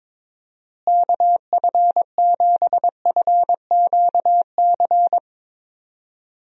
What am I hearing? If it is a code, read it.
Morse code, heard as a keyed tone, about 22 wpm, 700 Hz. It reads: KF7FQC
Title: USA Call Signs - 6 Characters - Repeated 2x - 22wpm